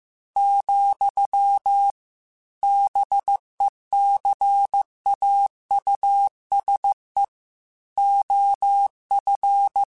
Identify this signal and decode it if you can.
Morse code, heard as a keyed tone, about 15 wpm, 800 Hz. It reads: , BECAUSE OF
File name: Technical Subjects / Technical Subjects-mp3s / OR Gate @13